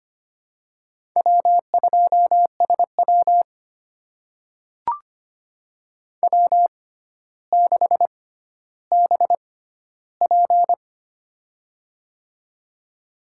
Morse code, heard as a keyed tone, about 25 wpm, spaced as for 13 wpm, 700 Hz. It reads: W2SW W6BP
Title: USA Call Signs - 4 Characters - 6x Character Spacing - 25wpm